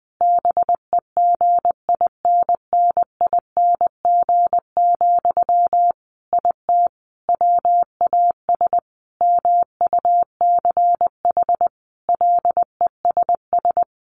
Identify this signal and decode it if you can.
Morse code, heard as a keyed tone, about 20 wpm, 700 Hz. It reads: BEGINNING, IT WAS MUCH LESS
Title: Sentences from Top 1400 Words 20wpm